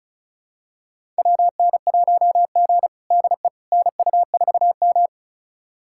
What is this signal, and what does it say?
Morse code, heard as a keyed tone, about 36 wpm, 700 Hz. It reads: WN1G DE NU4M
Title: USA Callsign DE Callsign - 4 Characters - 35wpm